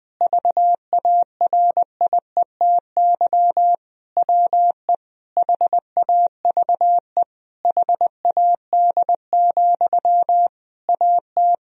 Morse code, heard as a keyed tone, about 20 wpm, 700 Hz. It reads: VARIETY WE HAVE HAD, AT
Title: Sentences from Top 1000 Words - Review Part 4 - 20wpm